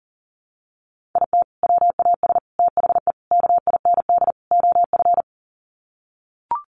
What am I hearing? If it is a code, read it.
Morse code, heard as a keyed tone, about 40 wpm, 700 Hz. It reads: IT WAS THE KIND OF
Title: Sets of 5 Words Encore2 40wpm